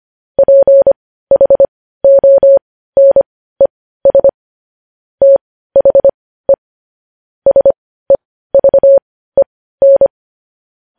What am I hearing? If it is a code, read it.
Morse code, heard as a keyed tone, about 25 wpm, spaced as for 15 wpm, 550 Hz. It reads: PHONES THE SEVEN
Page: news-headlines-in-morse-code-at-15-wpm